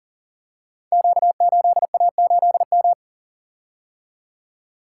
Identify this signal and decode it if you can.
Morse code, heard as a keyed tone, about 39 wpm, 700 Hz. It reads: Q8A8M
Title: Code Groups - 5 Letter:Numbers - 40wpm